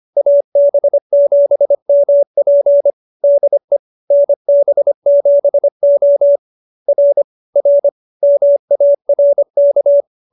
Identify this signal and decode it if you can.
Morse code, heard as a keyed tone, about 25 wpm, 590 Hz. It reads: AB7MP DE NB7O R R MARK
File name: QSO306 25